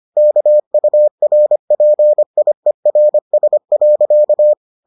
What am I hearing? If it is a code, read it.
Morse code, heard as a keyed tone, about 25 wpm, 600 Hz. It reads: KURPIERS.